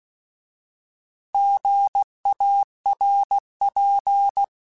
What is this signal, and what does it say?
Morse code, heard as a keyed tone, about 16 wpm, 800 Hz. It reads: GARP